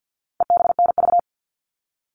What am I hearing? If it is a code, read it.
Morse code, heard as a keyed tone, about 50 wpm, 700 Hz. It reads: EBNV
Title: All Cut Numbers - 4-Digit 50wpm